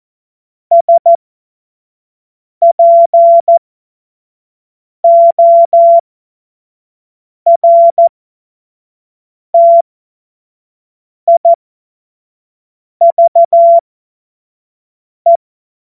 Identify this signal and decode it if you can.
Morse code, heard as a keyed tone, about 14 wpm, spaced as for 5 wpm, 680 Hz. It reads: SPORTIVE
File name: Morsekurs / Witze / Witz 27. April 2021, 25 F70